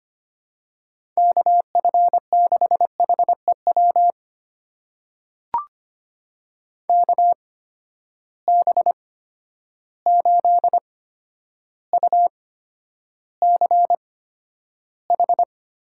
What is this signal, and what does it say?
Morse code, heard as a keyed tone, about 25 wpm, spaced as for 11 wpm, 700 Hz. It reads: KF6HEW KB8UCH
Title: USA Call Signs - 6 Characters - 8x Character Spacing - 25wpm